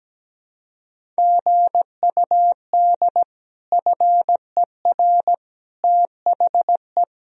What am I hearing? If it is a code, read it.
Morse code, heard as a keyed tone, about 17 wpm, 700 Hz. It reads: GUD FER THE
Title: Sentences from Top 300 Words and Abbreviations 17wpm